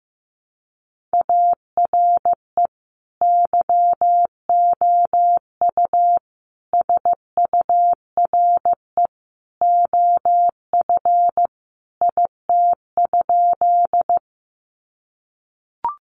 Morse code, heard as a keyed tone, about 15 wpm, 700 Hz. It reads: ARE YOU SURE OF IT?
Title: Sentences from Top 300 Words 15wpm